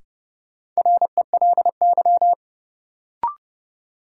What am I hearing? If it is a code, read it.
Morse code, heard as a keyed tone, about 32 wpm, 700 Hz. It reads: RELY
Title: Sentences from Top 1200 Words - New Word Drill - 30wpm